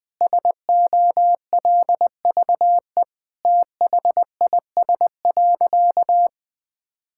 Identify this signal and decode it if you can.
Morse code, heard as a keyed tone, about 20 wpm, 700 Hz. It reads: SOLVE THIS.
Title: Sentences with 7-Word Phrases - 20wpm